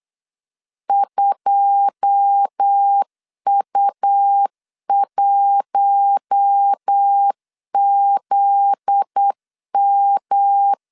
Morse code, heard as a keyed tone, about 8 wpm, 800 Hz. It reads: 2U1ZM
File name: LEZIONI / 9 2 8wpm 3437